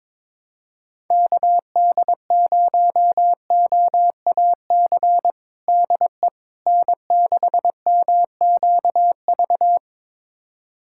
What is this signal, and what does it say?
Morse code, heard as a keyed tone, about 22 wpm, 700 Hz. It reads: KD0OAC DE N6MQV
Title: USA Callsign DE Callsign 22wpm